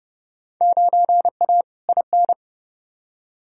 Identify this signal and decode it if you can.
Morse code, heard as a keyed tone, about 30 wpm, 700 Hz. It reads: 9A IN
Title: Field Day Exchanges Encore - 30wpm